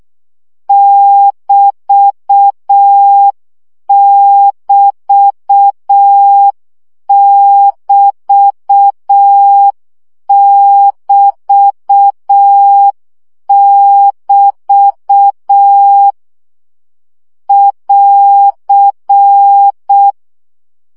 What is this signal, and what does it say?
Morse code, heard as a keyed tone, about 6 wpm, 800 Hz. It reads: <BT><BT><BT><BT><BT> <AR>
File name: CW gyakorlas egyenlo 5 baud